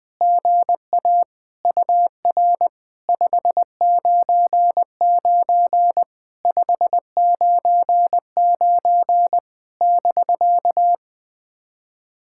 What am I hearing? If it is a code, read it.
Morse code, heard as a keyed tone, about 20 wpm, 700 Hz. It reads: GA UR 599 599 <BK>
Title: POTA QSO Practice - 20wpm